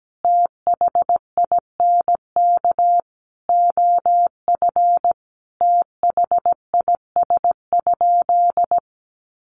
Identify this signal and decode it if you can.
Morse code, heard as a keyed tone, about 17 wpm, 700 Hz. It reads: THINK OF THIS?
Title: Sentences from Top 200 Words 17wpm